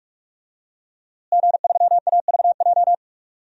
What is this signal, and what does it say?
Morse code, heard as a keyed tone, about 46 wpm, 700 Hz. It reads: G3AVJ
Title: DXCC-100 Call Signs - 5 Characters - Repeated 2x - 45wpm